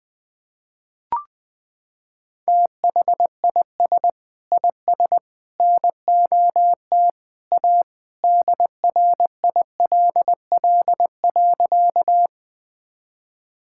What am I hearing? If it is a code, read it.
Morse code, heard as a keyed tone, about 20 wpm, 700 Hz. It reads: THIS IS NOT A DRILL.
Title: Sentences with 3-Word Phrases - 20wpm